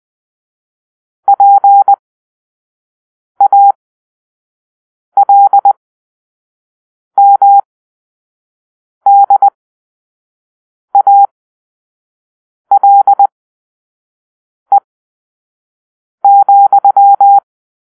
Morse code, heard as a keyed tone, about 20 wpm, spaced as for 5 wpm, 800 Hz. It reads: PALMDALE,
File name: arrl / Grp-6 Drill-4